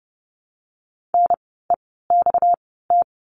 Morse code, heard as a keyed tone, about 30 wpm, spaced as for 17 wpm, 700 Hz. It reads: NEXT
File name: course / 30:10wpm / 237 Lesson 38f - Letter q - Words 30wpm